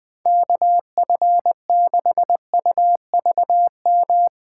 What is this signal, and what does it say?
Morse code, heard as a keyed tone, about 21 wpm, 700 Hz. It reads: KF6UVM
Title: USA Call Signs - 6 Characters - 3x Character Spacing - 20wpm